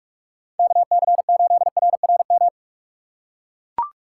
Morse code, heard as a keyed tone, about 44 wpm, 700 Hz. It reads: KC8RRM
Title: USA Call Signs - 6 Characters - 6x Character Spacing - 45wpm